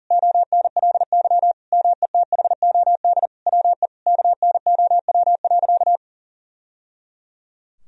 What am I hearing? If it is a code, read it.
Morse code, heard as a keyed tone, about 40 wpm, 700 Hz. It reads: ONLY METHOD WE KNOW.
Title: Sentences from Top 1000 Words - Review Part 3 - 40wpm